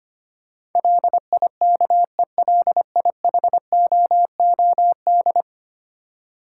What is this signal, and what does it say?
Morse code, heard as a keyed tone, about 25 wpm, 690 Hz. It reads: LIKELIHOOD
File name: 10 Letter Words - First 3 Letters Given - 25wpm